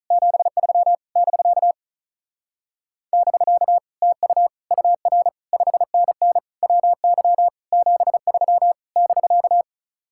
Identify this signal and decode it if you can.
Morse code, heard as a keyed tone, about 44 wpm, 700 Hz. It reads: 73 <BK> <BK> TU UR 5NN WY 73 <BK>